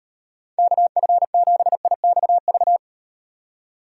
Morse code, heard as a keyed tone, about 36 wpm, 700 Hz. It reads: KF7IXV